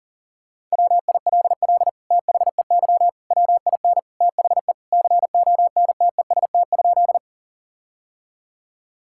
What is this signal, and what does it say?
Morse code, heard as a keyed tone, about 40 wpm, 700 Hz. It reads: WILL THEY WIN THE CONTEST?
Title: Sentences from Top 1600 Words 40wpm